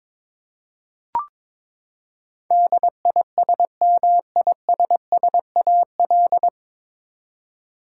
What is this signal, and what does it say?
Morse code, heard as a keyed tone, about 21 wpm, 700 Hz. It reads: DISMISSAL
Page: Morse Code Ninja practice 22wpm